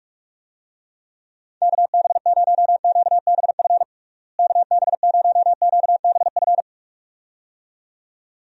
Morse code, heard as a keyed tone, about 46 wpm, 700 Hz. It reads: KB0QBF KB0QBF
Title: USA Call Signs - 6 Characters - Repeated 2x - 45wpm